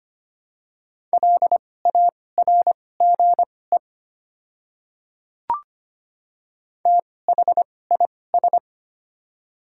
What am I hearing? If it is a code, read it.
Morse code, heard as a keyed tone, about 25 wpm, spaced as for 18 wpm, 700 Hz. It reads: LARGE THIS
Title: 2x Character Spacing - Top 300 Words - 25wpm